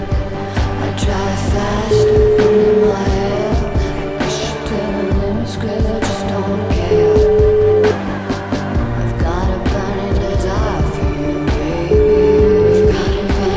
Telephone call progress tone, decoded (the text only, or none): ringback tone